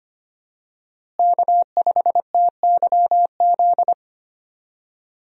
Morse code, heard as a keyed tone, about 25 wpm, 700 Hz. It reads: K5TYZ
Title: USA Call Signs - 5 Characters - Repeated 3x - 25wpm